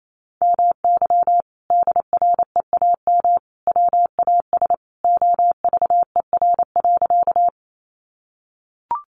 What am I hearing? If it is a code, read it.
Morse code, heard as a keyed tone, about 28 wpm, 700 Hz. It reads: MY DREAM WAS OVER.
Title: Sentences from Top 1500 Words 28wpm